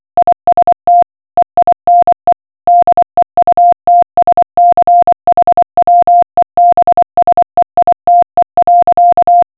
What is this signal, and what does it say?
Morse code, heard as a keyed tone, about 24 wpm, 700 Hz. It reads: IST EINE DEUTSCH WEBSEITE.